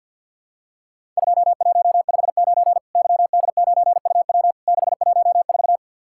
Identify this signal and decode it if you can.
Morse code, heard as a keyed tone, about 50 wpm, 700 Hz. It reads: 2159 YD9UW 614
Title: Worked All Europe DX Contest - QTC Exchange with Repeats - 50wpm